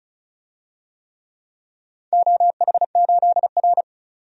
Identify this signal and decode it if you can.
Morse code, heard as a keyed tone, about 36 wpm, 700 Hz. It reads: OH8R
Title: DXCC-100 Call Signs - 4 Characters - 35wpm